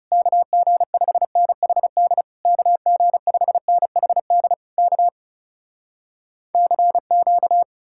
Morse code, heard as a keyed tone, about 37 wpm, 700 Hz. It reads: KG5NHD KG5NHD K CQ
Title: Speed-Racing Common QSO Phrases - 30wpm